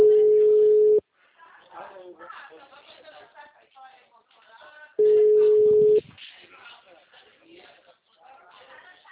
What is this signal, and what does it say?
Ringback tone (Europe, 425 Hz), with speech.